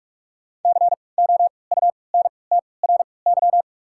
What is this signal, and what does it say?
Morse code, heard as a keyed tone, about 46 wpm, 700 Hz. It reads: C O U N T R Y
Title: 3x Character Spacing - Top 500 Words - 45wpm